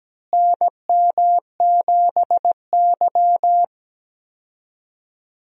Morse code, heard as a keyed tone, about 17 wpm, 700 Hz. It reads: NM7Y
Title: USA Call Signs - 4 Characters - Repeated 2x - 17wpm